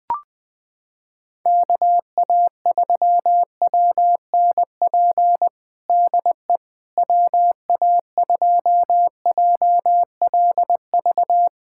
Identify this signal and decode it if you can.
Morse code, heard as a keyed tone, about 20 wpm, 700 Hz. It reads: KA3WNP DE WA2JLV
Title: USA Callsign DE Callsign - 6 Characters - 20wpm